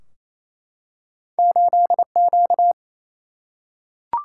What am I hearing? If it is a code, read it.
Morse code, heard as a keyed tone, about 28 wpm, 700 Hz. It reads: 8Q
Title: Single-Letter Single-Number 28wpm